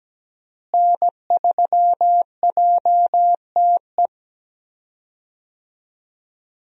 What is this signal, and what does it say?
Morse code, heard as a keyed tone, about 17 wpm, 700 Hz. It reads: N3JTE
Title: USA Call Signs - 5 Characters - Repeated 3x - 17wpm